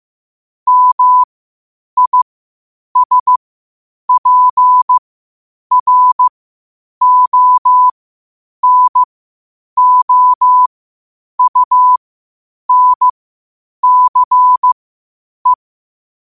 Morse code, heard as a keyed tone, about 15 wpm, spaced as for 8 wpm, 1000 Hz. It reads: MISPRONOUNCE
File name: qotdfiles / 8wpm